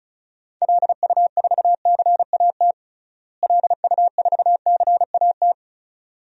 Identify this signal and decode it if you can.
Morse code, heard as a keyed tone, about 34 wpm, 700 Hz. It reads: LU4CAT LU4CAT
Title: DXCC-100 Call Signs - 6 Characters - Repeated 2x - 35wpm